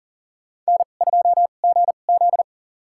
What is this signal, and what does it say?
Morse code, heard as a keyed tone, about 40 wpm, spaced as for 29 wpm, 700 Hz. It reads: N2GZ
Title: USA Call Signs - 2x Character Spacing - 40wpm